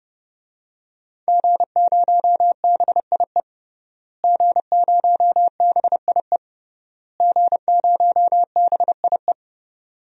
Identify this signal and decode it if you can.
Morse code, heard as a keyed tone, about 30 wpm, 700 Hz. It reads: G0BIE G0BIE G0BIE
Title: DXCC-100 Call Signs - 5 Characters - Repeated 3x - 30wpm